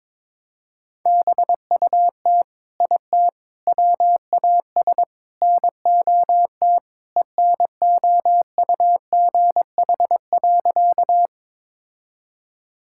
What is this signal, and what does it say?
Morse code, heard as a keyed tone, about 21 wpm, 700 Hz. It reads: BUT IT WAS NOT ENOUGH.
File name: practice / 22wpm / Sentences with 4-Word Phrases - 22wpm